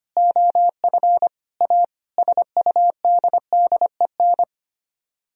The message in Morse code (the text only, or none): OF A SUDDEN